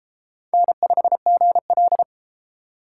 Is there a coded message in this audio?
Yes: Morse code from 0.5 to 2.0 s, heard as a keyed tone, about 28 wpm, 700 Hz. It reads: N5GL